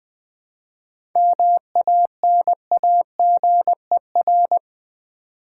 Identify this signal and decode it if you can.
Morse code, heard as a keyed tone, about 20 wpm, 700 Hz. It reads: MANAGER
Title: Seven Letter Words - 20wpm